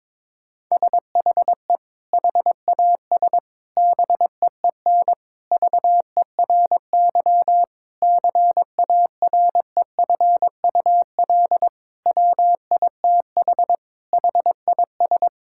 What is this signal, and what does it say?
Morse code, heard as a keyed tone, about 22 wpm, 700 Hz. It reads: SHE HAS BEEN VERY CAREFUL WITH HIS
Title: Sentences from Top 1600 Words 22wpm